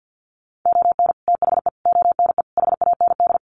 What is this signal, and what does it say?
Morse code, heard as a keyed tone, about 50 wpm, 700 Hz. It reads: ON THE ONE HAND
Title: Sets of 4 Words 50wpm